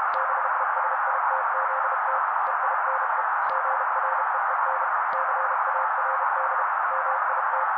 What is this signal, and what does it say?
Morse code, heard as a keyed tone, about 32 wpm, 530 Hz. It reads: BESITZT EINE GREEN CARD MIT
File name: daily / 31wpm noise